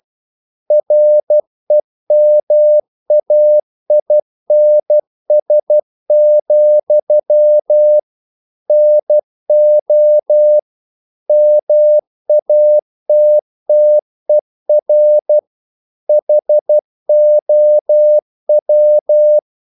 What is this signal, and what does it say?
Morse code, heard as a keyed tone, about 12 wpm, 600 Hz. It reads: REMAINS, NO MATTER HOW
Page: English quotations in CW (telegraphy - morse)